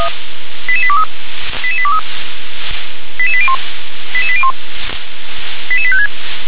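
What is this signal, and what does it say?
Five-tone selective call (ZVEI), heard as quick runs of five tones, tones of about 70 ms. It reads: 88823 88823 88801 88801 88856
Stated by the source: ZVEI five-tone selective call received on an SDR.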